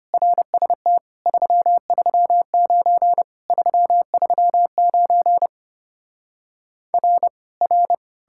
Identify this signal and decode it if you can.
Morse code, heard as a keyed tone, about 33 wpm, 700 Hz. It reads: RST 339 339 R R